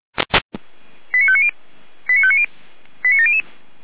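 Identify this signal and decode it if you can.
Five-tone selective call (ZVEI), heard as quick runs of five tones, tones of about 70 ms. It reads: 89590 89590 89700
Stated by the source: Five-tone ZVEI selective call received on an SDR.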